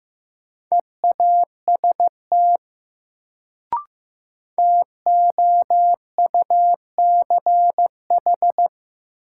Morse code, heard as a keyed tone, about 15 wpm, 700 Hz. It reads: EAST TOUCH
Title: Sentences from Top 1200 Words - New Word Drill Alt - 15wpm